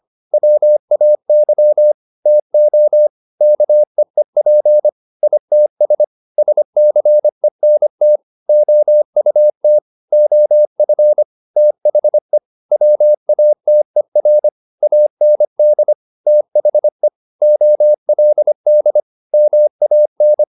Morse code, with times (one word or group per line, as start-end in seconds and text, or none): 0.3-1.9 WAY
2.3-3.1 TO
3.4-4.9 KEEP
5.2-6.0 ITS
6.4-8.2 SCENT
8.5-9.8 OUT
10.1-11.2 OF
11.6-12.4 THE
12.7-14.5 WATER
14.8-15.9 AND
16.3-17.1 THE
17.4-19.0 OLD
19.3-20.4 MAN